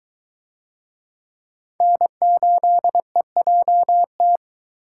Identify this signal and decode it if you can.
Morse code, heard as a keyed tone, about 22 wpm, 700 Hz. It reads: N8EJT